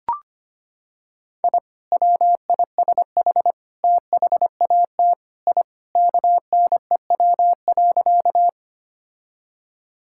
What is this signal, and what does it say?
Morse code, heard as a keyed tone, about 25 wpm, 690 Hz. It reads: I WISH THAT I KNEW.
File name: practice / 25wpm / Sentences with 2-Word Phrases - 25wpm